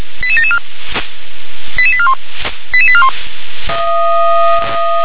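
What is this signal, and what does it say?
Five-tone selective call (ZVEI), heard as quick runs of five tones, tones of about 70 ms. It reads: 88774 88841 88841
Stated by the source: ZVEI five-tone selective call received on SDR.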